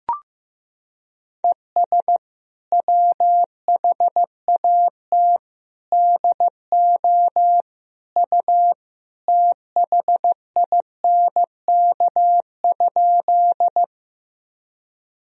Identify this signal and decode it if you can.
Morse code, heard as a keyed tone, about 16 wpm, 700 Hz. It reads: ES WHAT DO U THINK?
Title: Sentences from Top 200 Words and Abbreviations 15wpm